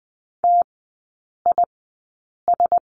Morse code, heard as a keyed tone, about 20 wpm, spaced as for 8 wpm, 700 Hz. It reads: TIS
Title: 047 Lesson 9e - Number 4 - Three Characters 20wpm